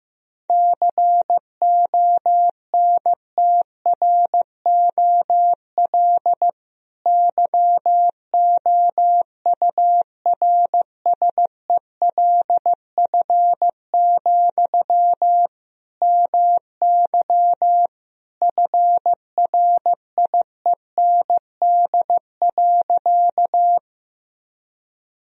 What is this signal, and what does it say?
Morse code, heard as a keyed tone, about 15 wpm, 700 Hz. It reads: CONTROL YOURSELF, MY FRIEND.
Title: Sentences from Top 1000 Words - Review Part 3 - 15wpm